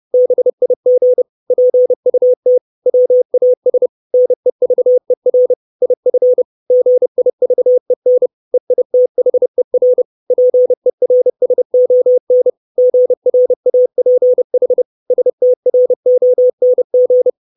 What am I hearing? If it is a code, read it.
Morse code, heard as a keyed tone, about 30 wpm, 500 Hz. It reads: BIG PUT WAS NEVER IF GIVEN EITHER PERSON GRAPH STRONG
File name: 632 Words 30WPM